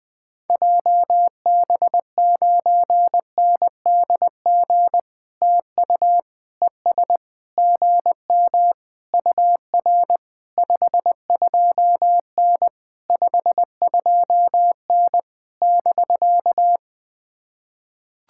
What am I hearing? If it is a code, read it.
Morse code, heard as a keyed tone, about 18 wpm, 700 Hz. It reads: JB9NDG TU ES GM UR 52N 52N <BK>
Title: POTA QSO Practice - 20wpm